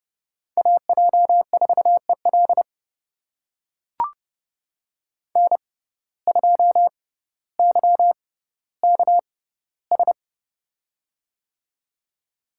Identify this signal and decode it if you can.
Morse code, heard as a keyed tone, about 31 wpm, spaced as for 16 wpm, 700 Hz. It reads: AJ4EL N2YKS
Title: USA Call Signs - 5 Characters - 6x Character Spacing - 30wpm